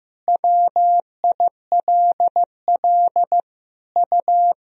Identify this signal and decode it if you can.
Morse code, heard as a keyed tone, about 15 wpm, 700 Hz. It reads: WILL U
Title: Sentences from Top 100 Words and Abbreviations 15wpm